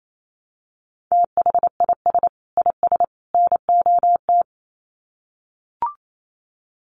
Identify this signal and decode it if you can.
Morse code, heard as a keyed tone, about 28 wpm, 700 Hz. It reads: THIS IS NOT
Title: Sets of 3 Words 28wpm